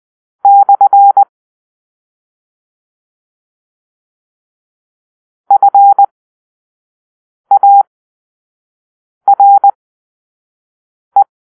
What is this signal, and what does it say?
Morse code, heard as a keyed tone, about 20 wpm, spaced as for 5 wpm, 800 Hz. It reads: / FARE
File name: arrl / Grp-4 Drill-1